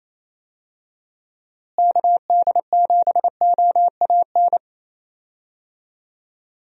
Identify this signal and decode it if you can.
Morse code, heard as a keyed tone, about 28 wpm, 700 Hz. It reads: KD7OAN
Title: USA Call Signs - 6 Characters - Repeated 3x - 28wpm